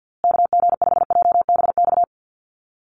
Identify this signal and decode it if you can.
Morse code, heard as a keyed tone, about 50 wpm, 700 Hz. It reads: KG5JBX
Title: USA Call Signs Repeated 3x 50wpm